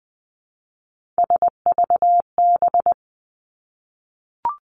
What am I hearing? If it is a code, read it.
Morse code, heard as a keyed tone, about 20 wpm, 700 Hz. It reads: SVB
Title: Three Challenging Letter-Number Combinations 20wpm